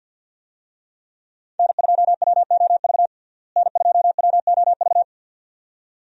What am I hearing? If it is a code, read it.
Morse code, heard as a keyed tone, about 47 wpm, 700 Hz. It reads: N2WOV N2WOV
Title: USA Call Signs - 5 Characters - Repeated 2x - 50wpm